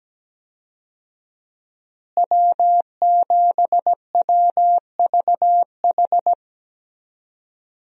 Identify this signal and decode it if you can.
Morse code, heard as a keyed tone, about 17 wpm, 700 Hz. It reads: W7WVH
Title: USA Call Signs - 5 Characters - Repeated 2x - 17wpm